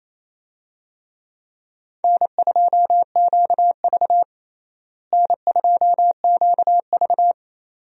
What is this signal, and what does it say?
Morse code, heard as a keyed tone, about 28 wpm, 700 Hz. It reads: N2QV N2QV
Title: USA Call Signs - 4 Characters - Repeated 2x - 28wpm